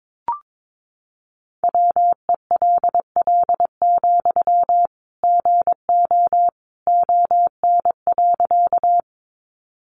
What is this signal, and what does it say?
Morse code, heard as a keyed tone, about 22 wpm, 700 Hz. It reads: WELL, GO ON.
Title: Sentences from Top 200 Words 22wpm